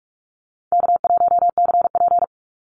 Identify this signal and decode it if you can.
Morse code, heard as a keyed tone, about 44 wpm, 700 Hz. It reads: K1CP
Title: Call Signs Repeated 2x - 45wpm